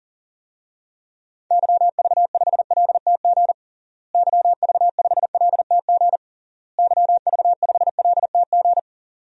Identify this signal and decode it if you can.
Morse code, heard as a keyed tone, about 40 wpm, 700 Hz. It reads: YV5LTG YV5LTG YV5LTG
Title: DXCC-100 Call Signs - 6 Characters - Repeated 3x - 40wpm